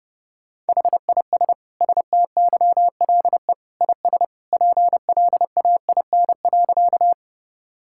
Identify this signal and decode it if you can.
Morse code, heard as a keyed tone, about 30 wpm, 700 Hz. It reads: HIS STYLE IS PLAIN.
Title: Sentences from Top 1000 Words - Review Part 5 - 30wpm